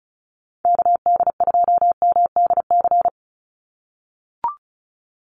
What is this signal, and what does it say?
Morse code, heard as a keyed tone, about 34 wpm, 700 Hz. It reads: KD2MDC
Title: USA Call Signs 35wpm